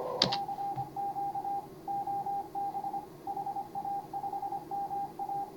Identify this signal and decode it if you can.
Morse code, heard as a keyed tone, about 25 wpm, 810 Hz. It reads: CQ OB HA5MA